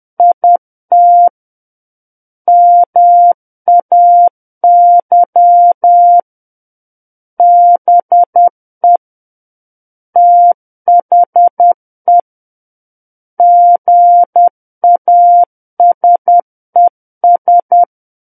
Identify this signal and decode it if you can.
Morse code, heard as a keyed tone, about 10 wpm, 700 Hz. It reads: IT MAY BE THE GASES
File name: Book One / 01-Chapter 1 - 10 WPM